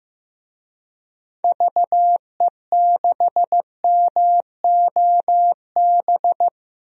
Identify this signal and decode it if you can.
Morse code, heard as a keyed tone, about 15 wpm, 700 Hz. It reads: VE6MOB
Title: DXCC-100 Call Signs - 6 Characters - 15wpm